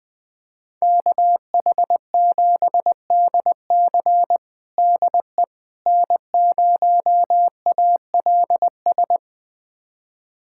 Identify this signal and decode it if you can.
Morse code, heard as a keyed tone, about 21 wpm, 700 Hz. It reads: KH7DC DE N0ALS
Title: USA Callsign DE Callsign - 5 Characters - 20wpm